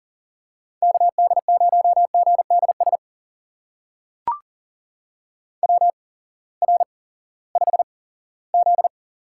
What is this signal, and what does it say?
Morse code, heard as a keyed tone, about 38 wpm, spaced as for 20 wpm, 700 Hz. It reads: KD0GDS WR5Z
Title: USA Call Signs - 8x Character Spacing - 40wpm